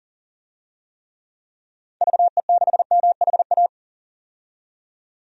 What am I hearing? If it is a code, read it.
Morse code, heard as a keyed tone, about 40 wpm, 700 Hz. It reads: VE6MHA
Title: DXCC-100 Call Signs - 6 Characters - 40wpm